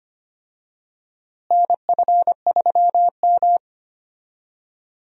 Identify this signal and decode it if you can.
Morse code, heard as a keyed tone, about 25 wpm, 700 Hz. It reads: NF3M
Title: USA Call Signs - 4 Characters - Repeated 2x - 25wpm